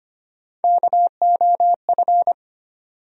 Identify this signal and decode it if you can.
Morse code, heard as a keyed tone, about 25 wpm, 700 Hz. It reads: KOF
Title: USA Call Sign Postfixes - 3 Letters 25wpm